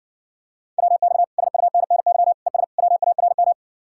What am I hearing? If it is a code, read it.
Morse code, heard as a keyed tone, about 60 wpm, 700 Hz. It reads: WX SUNNY ES WARM